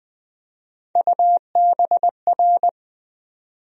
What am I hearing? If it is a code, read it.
Morse code, heard as a keyed tone, about 21 wpm, 700 Hz. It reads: UBR
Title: USA Call Sign Postfixes - 3 Letters 20wpm